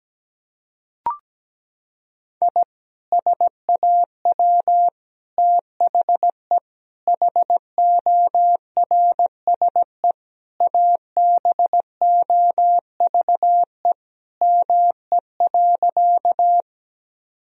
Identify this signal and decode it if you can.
Morse code, heard as a keyed tone, about 17 wpm, 700 Hz. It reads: I SAW THE HORSE ABOVE ME.
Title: Sentences from Top 1000 Words - Review Part 2 - 17wpm